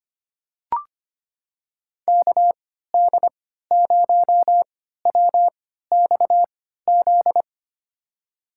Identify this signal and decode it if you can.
Morse code, heard as a keyed tone, about 25 wpm, spaced as for 14 wpm, 700 Hz. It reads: KD0WXZ
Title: USA Call Signs - 6 Characters - 3x Character Spacing - 25wpm